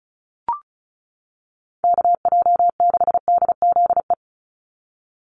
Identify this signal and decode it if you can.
Morse code, heard as a keyed tone, about 34 wpm, 700 Hz. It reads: KJ6DZE
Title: USA Call Signs 35wpm